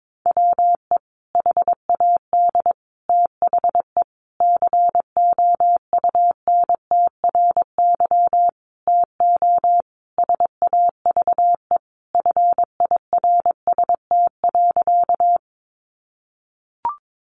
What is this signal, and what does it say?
Morse code, heard as a keyed tone, about 22 wpm, 700 Hz. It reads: WE HAD THE COUNTRY TO SAVE FIRST.
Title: Sentences from Top 1100 Words 22wpm